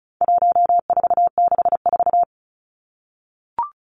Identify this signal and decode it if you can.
Morse code, heard as a keyed tone, about 34 wpm, 700 Hz. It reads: 1464
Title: Numbers - 4-Digit 35wpm